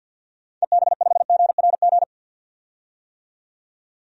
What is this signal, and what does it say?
Morse code, heard as a keyed tone, about 52 wpm, 700 Hz. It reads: EB5GRG
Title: DXCC-100 Call Signs - 6 Characters - 50wpm